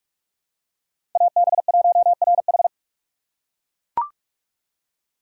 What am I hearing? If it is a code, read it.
Morse code, heard as a keyed tone, about 41 wpm, 700 Hz. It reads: AB1RH
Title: USA Call Signs - 3x Character Spacing - 45wpm